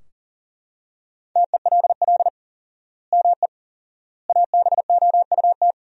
Morse code, heard as a keyed tone, about 40 wpm, 700 Hz. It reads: TELL ME ABOUT
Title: 4x Word-Spacing - Sentences from Top 200 Words - 40wpm